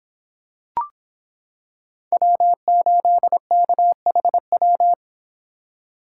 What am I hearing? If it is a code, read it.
Morse code, heard as a keyed tone, about 25 wpm, 700 Hz. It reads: W8KHW